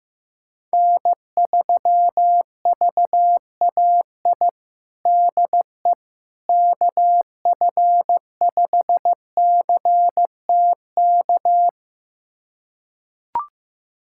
Morse code, heard as a keyed tone, about 15 wpm, 700 Hz. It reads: N3VAI DE KF5CTK
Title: USA Callsign DE Callsign 15wpm